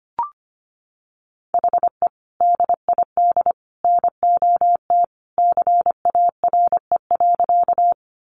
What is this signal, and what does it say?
Morse code, heard as a keyed tone, about 25 wpm, 700 Hz. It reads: HE DID NOT CARE.
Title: Sentences from Top 600 Words 25wpm